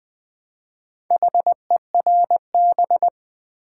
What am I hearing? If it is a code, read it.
Morse code, heard as a keyed tone, about 20 wpm, 700 Hz. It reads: HERB
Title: Build A Word - 1 Letter At A Time - 4-Letter Words - 20wpm